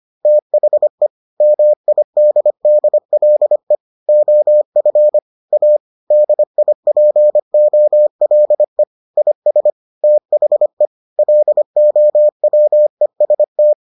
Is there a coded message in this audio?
Yes: Morse code from 0.3 to 13.7 s, heard as a keyed tone, about 25 wpm, 600 Hz. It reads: THE MIDDLE OF A DIPOLE IS THE LOWEST